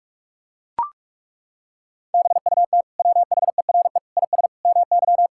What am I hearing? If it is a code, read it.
Morse code, heard as a keyed tone, about 42 wpm, 700 Hz. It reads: BUT WHERE IS MY
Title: Sentences from Top 1600 Words 45wpm